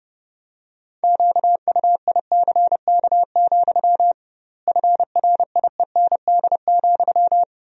Morse code, heard as a keyed tone, about 30 wpm, 700 Hz. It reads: QUICK, FRIEND,
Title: 2x Word-Spacing - Sentences from Top 500 Words - 30wpm